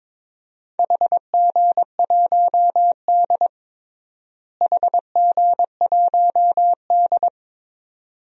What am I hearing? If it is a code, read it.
Morse code, heard as a keyed tone, about 22 wpm, 700 Hz. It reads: HG1D HG1D
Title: DXCC-100 Call Signs - 4 Characters - Repeated 2x - 22wpm